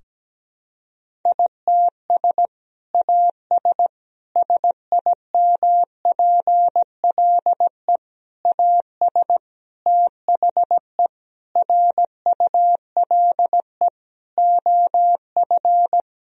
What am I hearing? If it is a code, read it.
Morse code, heard as a keyed tone, about 17 wpm, 700 Hz. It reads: ITS AS SIMPLE AS THE RULE OF
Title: Sentences from Top 1000 Words - Review Part 2 - 17wpm